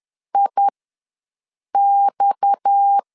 Morse code, heard as a keyed tone, about 10 wpm, spaced as for 6 wpm, 800 Hz. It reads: IX